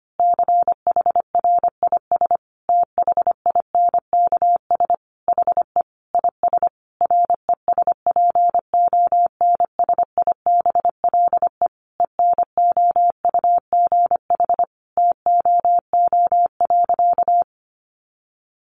Morse code, heard as a keyed tone, about 25 wpm, 700 Hz. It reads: CHRIS THINKS HE IS RESPONSIBLE ENOUGH TOO.
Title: Easy Phrases 25wpm